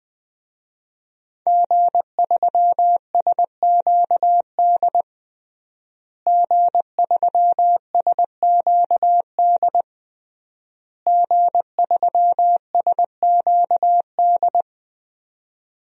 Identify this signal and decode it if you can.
Morse code, heard as a keyed tone, about 20 wpm, 700 Hz. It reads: G3SQD G3SQD G3SQD
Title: DXCC-100 Call Signs - 5 Characters - Repeated 3x - 20wpm